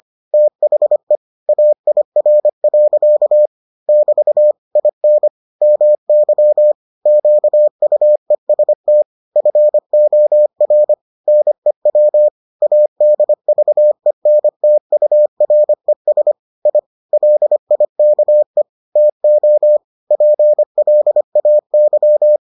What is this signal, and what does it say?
Morse code, heard as a keyed tone, about 25 wpm, 600 Hz. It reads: THE AIR. <BT> IN MY QUEST FOR NEW ADVENTURES I LIKE TO PLAY